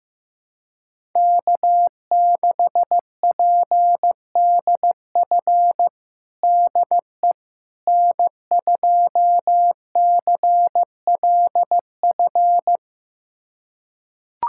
Morse code, heard as a keyed tone, about 15 wpm, 700 Hz. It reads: K6PDF DE N2CLF
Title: USA Callsign DE Callsign - 5 Characters - 15wpm